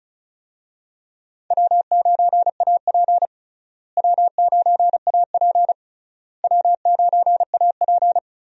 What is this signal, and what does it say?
Morse code, heard as a keyed tone, about 36 wpm, 700 Hz. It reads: W9AP W9AP W9AP
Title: USA Call Signs - 4 Characters - Repeated 3x - 35wpm